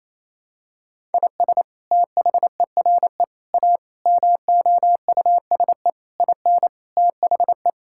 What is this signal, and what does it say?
Morse code, heard as a keyed tone, about 27 wpm, 700 Hz. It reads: IS THERE A MOUSE IN THE
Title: Sentences from Top 1000 Words - Review Part 5 - 28wpm